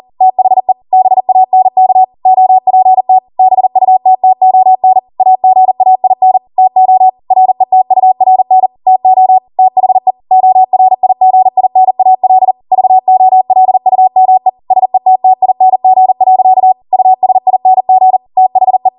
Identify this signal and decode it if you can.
Morse code, heard as a keyed tone, about 40 wpm, 750 Hz. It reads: THE BANK OPT BUTTON AGAIN TO RETURN TO THE ORIGINAL VOLUME SETTING. USING THE
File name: Morse / 40WPM